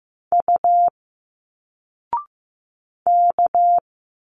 Morse code, heard as a keyed tone, about 15 wpm, 700 Hz. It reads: U K
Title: Single Letters 15wpm